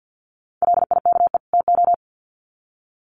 Morse code, heard as a keyed tone, about 50 wpm, 700 Hz. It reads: LIKE TO
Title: Sets of 2 Words 50wpm